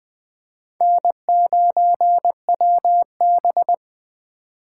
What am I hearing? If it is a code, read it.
Morse code, heard as a keyed tone, about 20 wpm, 700 Hz. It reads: N9WB
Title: USA Call Signs - 4 Characters - 3x Character Spacing - 20wpm